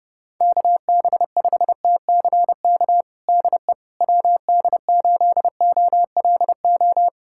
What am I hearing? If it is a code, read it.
Morse code, heard as a keyed tone, about 30 wpm, 700 Hz. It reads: KB5TCK DE WD8OLO
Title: USA Callsign DE Callsign - 6 Characters - 30wpm